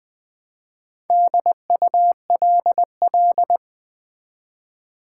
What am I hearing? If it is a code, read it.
Morse code, heard as a keyed tone, about 20 wpm, 700 Hz. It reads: DULL
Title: Build A Word - 1 Letter At A Time - 4-Letter Words - 20wpm